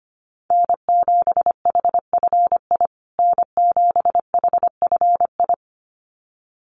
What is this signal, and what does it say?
Morse code, heard as a keyed tone, about 25 wpm, 700 Hz. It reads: N7HFI N7HFI
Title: USA Call Signs Repeated 2x 25wpm